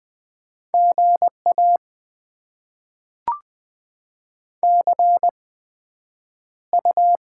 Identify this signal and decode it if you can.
Morse code, heard as a keyed tone, about 20 wpm, spaced as for 9 wpm, 700 Hz. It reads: GA CU
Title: Top 20 QSO Elements - 8x Character Spacing - 20wpm